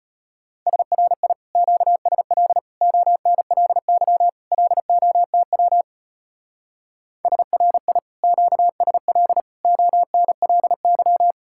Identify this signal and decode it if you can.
Morse code, heard as a keyed tone, about 33 wpm, 700 Hz. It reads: SRI QSL ONLY LOTW SRI QSL ONLY